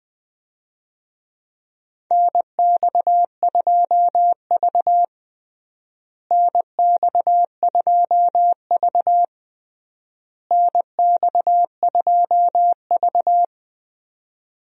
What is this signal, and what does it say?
Morse code, heard as a keyed tone, about 20 wpm, 700 Hz. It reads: NX2V NX2V NX2V
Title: USA Call Signs - 4 Characters - Repeated 3x - 20wpm